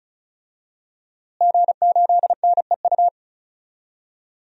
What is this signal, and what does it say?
Morse code, heard as a keyed tone, about 36 wpm, 700 Hz. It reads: G8NEU
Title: DXCC-100 Call Signs - 5 Characters - Repeated 2x - 35wpm